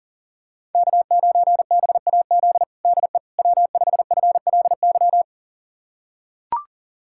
Morse code, heard as a keyed tone, about 40 wpm, 700 Hz. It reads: K9BAZ DE W5FLY
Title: USA Callsign DE Callsign - 5 Characters - 40wpm